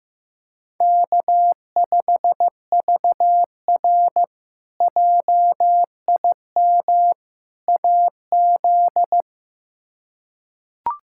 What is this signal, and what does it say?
Morse code, heard as a keyed tone, about 15 wpm, 700 Hz. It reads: K5VR JIM AZ
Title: K1USN SST QSO Practice - 15wpm